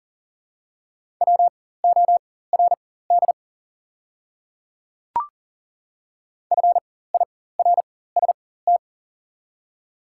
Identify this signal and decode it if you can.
Morse code, heard as a keyed tone, about 40 wpm, spaced as for 19 wpm, 690 Hz. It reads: WORD FIRST
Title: 4x Character Spacing - Top 100 Words - 40wpm